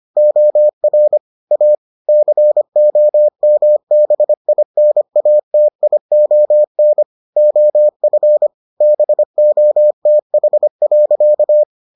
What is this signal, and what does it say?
Morse code, heard as a keyed tone, about 25 wpm, 600 Hz. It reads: OR A COMBINATION OF BOTH.